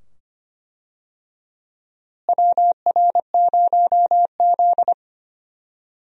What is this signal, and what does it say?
Morse code, heard as a keyed tone, about 25 wpm, 700 Hz. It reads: WR0Z
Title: USA Call Signs - 4 Characters - Repeated 2x - 25wpm